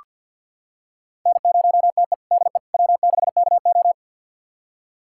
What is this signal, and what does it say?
Morse code, heard as a keyed tone, about 48 wpm, 700 Hz. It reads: N0TE DE W6KO
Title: USA Callsign DE Callsign - 4 Characters - 50wpm